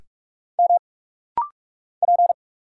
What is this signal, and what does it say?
Morse code, heard as a keyed tone, about 44 wpm, 700 Hz. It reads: M P
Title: Single Letter-Number - Rapid-Fire 45wpm